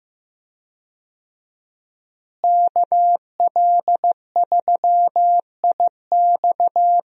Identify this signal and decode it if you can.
Morse code, heard as a keyed tone, about 15 wpm, 700 Hz. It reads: KL3IX
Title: USA Call Signs - 5 Characters - Repeated 3x - 15wpm